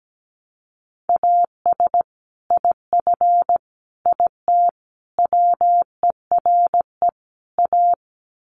Morse code, heard as a keyed tone, about 17 wpm, 700 Hz. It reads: AS IF IT WERE A
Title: Sets of 5 Words 17wpm